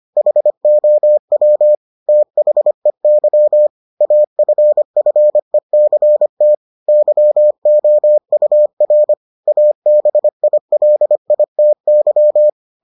Morse code, heard as a keyed tone, about 25 wpm, 600 Hz. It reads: HOW THEY AFFECT YOUR ABILITY